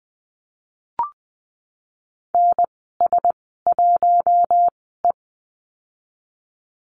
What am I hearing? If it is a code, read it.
Morse code, heard as a keyed tone, about 19 wpm, spaced as for 14 wpm, 700 Hz. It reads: NS1E
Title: 048 Lesson 9g - Number 4 - Call Signs 20wpm